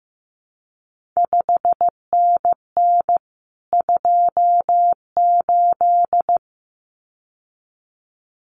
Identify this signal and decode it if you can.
Morse code, heard as a keyed tone, about 15 wpm, 700 Hz. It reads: 5NN 28
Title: CQ WW DX Contest Exchange Encore - 15wpm